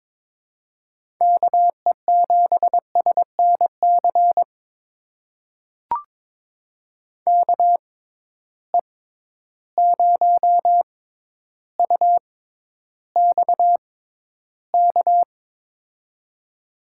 Morse code, heard as a keyed tone, about 22 wpm, spaced as for 11 wpm, 700 Hz. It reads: KE7SNC KE0UXK
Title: USA Call Signs - 6 Characters - 6x Character Spacing - 22wpm